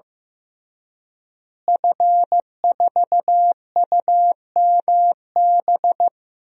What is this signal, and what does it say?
Morse code, heard as a keyed tone, about 15 wpm, 700 Hz. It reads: F4UMB
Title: DXCC-100 Call Signs - 5 Characters - Repeated 3x - 15wpm